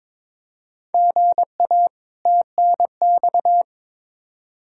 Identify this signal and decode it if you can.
Morse code, heard as a keyed tone, about 22 wpm, 700 Hz. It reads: GA TNX
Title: Practice QSO - Exchange 1 - Beginning - 22wpm